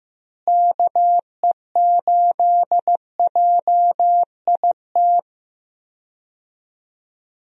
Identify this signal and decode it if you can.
Morse code, heard as a keyed tone, about 15 wpm, 700 Hz. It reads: KE8JIT
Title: USA Call Signs - 6 Characters - Repeated 2x - 15wpm